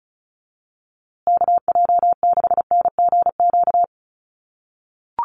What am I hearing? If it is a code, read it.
Morse code, heard as a keyed tone, about 34 wpm, 700 Hz. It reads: KJ6NGQ
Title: USA Call Signs Repeated 2x 35wpm